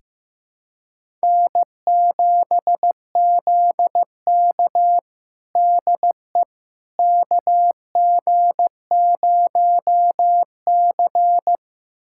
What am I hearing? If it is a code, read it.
Morse code, heard as a keyed tone, about 15 wpm, 700 Hz. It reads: N7ZK DE KG0C
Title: USA Callsign DE Callsign - 4 Characters - 15wpm